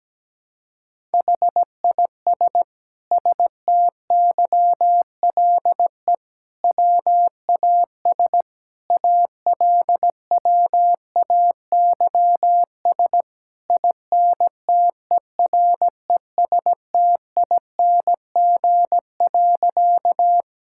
Morse code, heard as a keyed tone, about 17 wpm, 700 Hz. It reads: HIS STYLE WAS ALWAYS INTERESTING.
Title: Sentences from Top 1000 Words - Review Part 5 - 17wpm